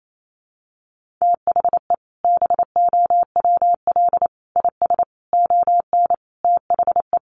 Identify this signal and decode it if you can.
Morse code, heard as a keyed tone, about 28 wpm, 700 Hz. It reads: THE BOWL IS ON THE
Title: Sentences from Top 1200 Words 28wpm